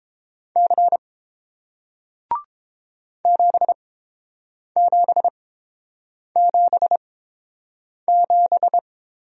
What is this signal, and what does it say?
Morse code, heard as a keyed tone, about 30 wpm, spaced as for 7 wpm, 700 Hz. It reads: C 7777